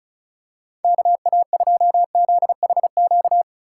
Morse code, heard as a keyed tone, about 34 wpm, 700 Hz. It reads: KA2ZHQ
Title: USA Call Signs - 3x Character Spacing - 35wpm